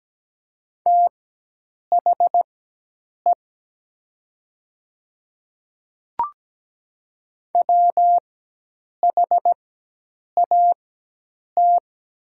Morse code, heard as a keyed tone, about 17 wpm, spaced as for 8 wpm, 700 Hz. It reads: THE WHAT
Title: 4x Character Spacing - Top 200 Words - 17wpm